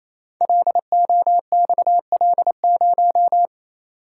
Morse code, heard as a keyed tone, about 28 wpm, 700 Hz. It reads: LOXL0
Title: Code Groups - 5 Letter:Numbers - 28wpm